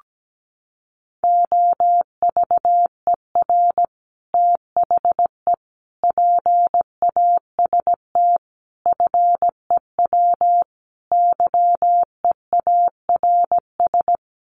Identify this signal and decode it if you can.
Morse code, heard as a keyed tone, about 17 wpm, 700 Hz. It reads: OVER THE PAST FEW YEARS
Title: Sets of 5 Words Encore 17wpm